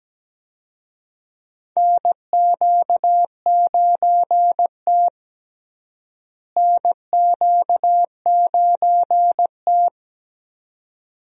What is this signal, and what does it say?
Morse code, heard as a keyed tone, about 17 wpm, 700 Hz. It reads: NQ9T NQ9T
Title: USA Call Signs - 4 Characters - Repeated 2x - 17wpm